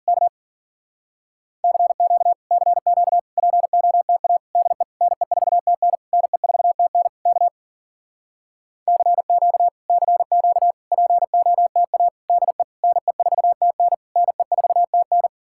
Morse code, heard as a keyed tone, about 52 wpm, 700 Hz. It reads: K CQ CQ POTA DE NE4TN NE4TN K CQ CQ POTA DE NE4TN NE4TN